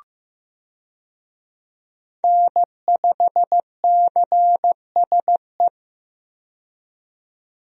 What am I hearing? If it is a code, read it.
Morse code, heard as a keyed tone, about 15 wpm, 700 Hz. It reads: N5CSE
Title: USA Call Signs - 5 Characters - 15wpm